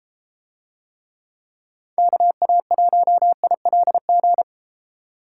Morse code, heard as a keyed tone, about 31 wpm, 700 Hz. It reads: KA1ILG